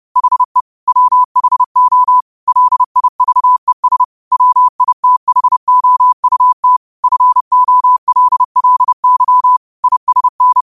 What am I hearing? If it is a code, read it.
Morse code, heard as a keyed tone, about 30 wpm, 1000 Hz. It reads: HE WHO LIVES WITHOUT FOLLY ISN